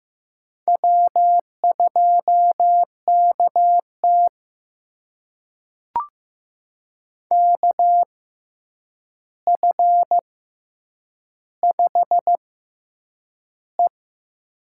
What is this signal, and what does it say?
Morse code, heard as a keyed tone, about 15 wpm, spaced as for 8 wpm, 700 Hz. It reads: W2KT KF5E
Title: USA Call Signs - 4 Characters - 6x Character Spacing - 15wpm